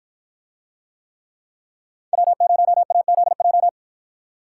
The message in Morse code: W0AZJ

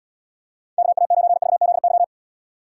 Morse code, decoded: KA9VZY

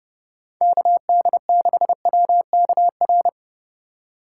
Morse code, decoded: KD6WKR

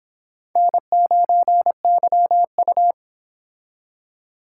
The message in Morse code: N9YU

